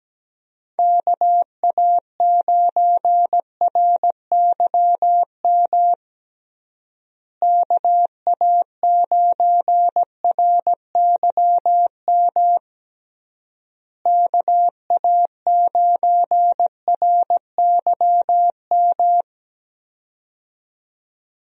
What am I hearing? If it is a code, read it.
Morse code, heard as a keyed tone, about 17 wpm, 700 Hz. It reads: KA9RYM KA9RYM KA9RYM
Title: USA Call Signs - 6 Characters - Repeated 3x - 17wpm